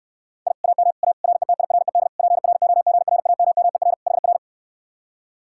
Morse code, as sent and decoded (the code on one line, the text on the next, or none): .. / .- -- / .- / .-. - .. .-. -.. / .--. .-. --- --. .-. .- -- -- . .-. / .... .-.
I AM A RETIRED PROGRAMMER HR